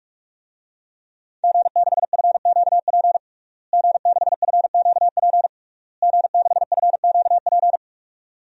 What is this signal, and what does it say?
Morse code, heard as a keyed tone, about 46 wpm, 700 Hz. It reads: G6FQP G6FQP G6FQP
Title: DXCC-100 Call Signs - 5 Characters - Repeated 3x - 45wpm